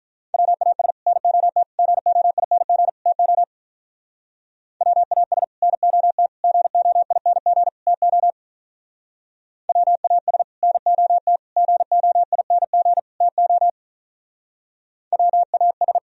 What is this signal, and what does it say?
Morse code, heard as a keyed tone, about 45 wpm, 700 Hz. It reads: WAS NOT GOING TO WAS NOT GOING TO WAS NOT GOING TO WAS